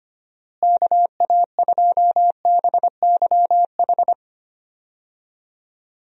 Morse code, heard as a keyed tone, about 25 wpm, 700 Hz. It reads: KA2BYH
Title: USA Call Signs - 6 Characters - Repeated 2x - 25wpm